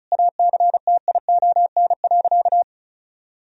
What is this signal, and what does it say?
Morse code, heard as a keyed tone, about 36 wpm, 700 Hz. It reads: ACTION.
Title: Sentences from Top 1000 Words - Review Part 4 - 35wpm